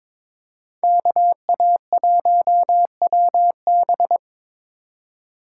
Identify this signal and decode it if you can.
Morse code, heard as a keyed tone, about 26 wpm, 700 Hz. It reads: KA1WB
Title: Speed Racing - USA 2x2 Call Signs 22wpm